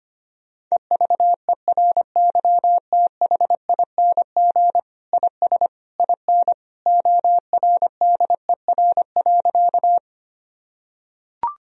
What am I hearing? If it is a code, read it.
Morse code, heard as a keyed tone, about 25 wpm, 700 Hz. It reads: EVERYTHING IS IN ORDER.
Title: Sentences from Top 1000 Words - Review Part 2 - 25wpm